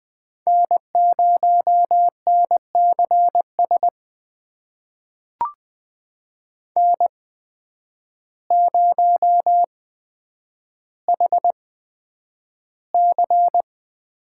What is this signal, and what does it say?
Morse code, heard as a keyed tone, about 20 wpm, spaced as for 9 wpm, 700 Hz. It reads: N0NCS N0HC
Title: USA Call Signs - 5 Characters - 8x Character Spacing - 20wpm